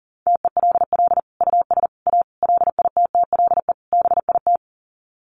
Morse code, heard as a keyed tone, about 40 wpm, 700 Hz. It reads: TELL US A LITTLE BIT
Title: Sets of 5 Words Encore2 40wpm